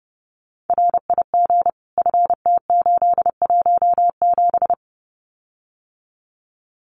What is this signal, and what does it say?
Morse code, heard as a keyed tone, about 30 wpm, 700 Hz. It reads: RIG FT817
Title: Common QSO Phrases Encore 30wpm